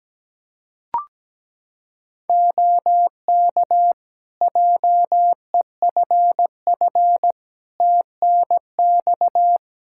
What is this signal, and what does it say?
Morse code, heard as a keyed tone, about 17 wpm, 700 Hz. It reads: OK JEFF TNX
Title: Practice QSO - Exchange 4 - Ending - 17wpm